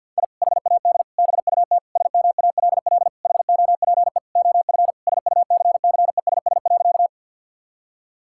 Morse code, heard as a keyed tone, about 49 wpm, 700 Hz. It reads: I HAD BUT SMALL HOPE OF SUCCESS.
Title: Sentences from Top 1000 Words - Review Part 6 - 50wpm